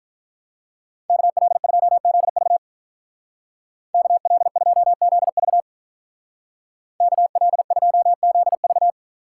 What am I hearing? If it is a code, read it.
Morse code, heard as a keyed tone, about 44 wpm, 700 Hz. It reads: KL2ZV KL2ZV KL2ZV